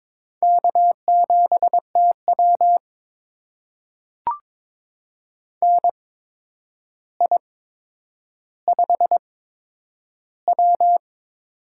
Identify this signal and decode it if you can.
Morse code, heard as a keyed tone, about 22 wpm, spaced as for 10 wpm, 700 Hz. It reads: K7TW NI5W
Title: USA Call Signs - 4 Characters - 8x Character Spacing - 22wpm